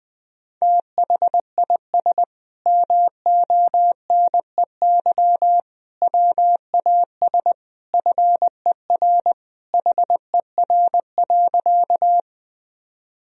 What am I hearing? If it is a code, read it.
Morse code, heard as a keyed tone, about 20 wpm, 700 Hz. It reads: THIS MONEY WAS FER HER.
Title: Sentences from Top 400 Words and Abbreviations 20wpm